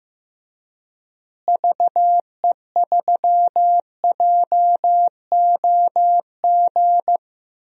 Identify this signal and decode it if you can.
Morse code, heard as a keyed tone, about 15 wpm, 700 Hz. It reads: VE3JOG
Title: DXCC-100 Call Signs - 6 Characters - Repeated 3x - 15wpm